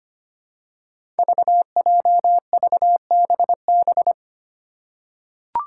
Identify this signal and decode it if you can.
Morse code, heard as a keyed tone, about 25 wpm, 700 Hz. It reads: VJVBB
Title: Five Challenging Letter-Number Combinations 25wpm